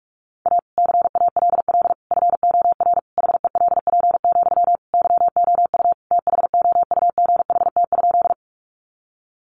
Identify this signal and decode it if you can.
Morse code, heard as a keyed tone, about 44 wpm, 700 Hz. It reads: A CALL FOR HELP, YOU THOUGHT?
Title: Sentences from Top 300 Words 45wpm